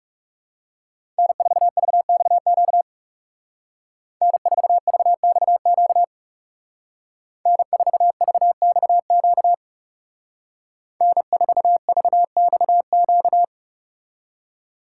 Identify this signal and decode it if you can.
Morse code, heard as a keyed tone, about 39 wpm, 700 Hz. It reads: N4VXQ N4VXQ N4VXQ N4VXQ